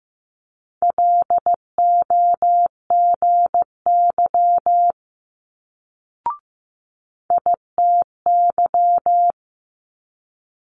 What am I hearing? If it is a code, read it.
Morse code, heard as a keyed tone, about 15 wpm, 700 Hz. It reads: LOGY ITY
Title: Multi-letter Phonemes Quick Fire 15wpm